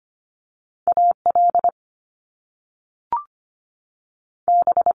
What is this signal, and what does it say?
Morse code, heard as a keyed tone, about 25 wpm, 700 Hz. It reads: A L B
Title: Top Two Letter Combinations in Words 25wpm